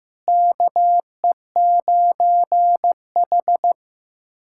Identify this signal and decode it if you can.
Morse code, heard as a keyed tone, about 16 wpm, 700 Hz. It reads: KE9H
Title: USA Callsign DE Callsign - 4 Characters - 15wpm